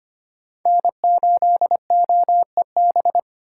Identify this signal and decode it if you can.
Morse code, heard as a keyed tone, about 25 wpm, 700 Hz. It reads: N8OEB
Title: USA Call Signs - 5 Characters - 2x Character Spacing - 25wpm